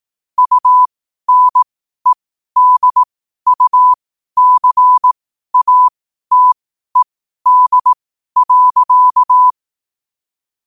Morse code, heard as a keyed tone, about 18 wpm, spaced as for 12 wpm, 1000 Hz. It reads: UNEDUCATED.
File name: qotdfiles / 12wpm